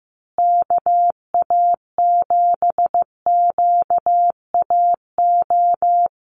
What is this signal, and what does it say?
Morse code, heard as a keyed tone, about 15 wpm, 700 Hz. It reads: KA7QAO
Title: USA Call Signs Repeated 2x 15wpm